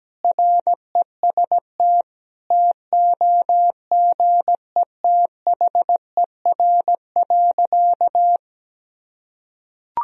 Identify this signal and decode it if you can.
Morse code, heard as a keyed tone, about 17 wpm, 700 Hz. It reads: REST TOGETHER.
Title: Sentences from Top 500 Words and Abbreviations 17wpm